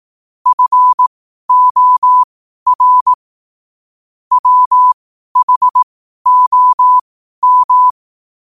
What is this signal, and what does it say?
Morse code, heard as a keyed tone, about 18 wpm, spaced as for 12 wpm, 1000 Hz. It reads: FOR WHOM
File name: qotdfiles / 12wpm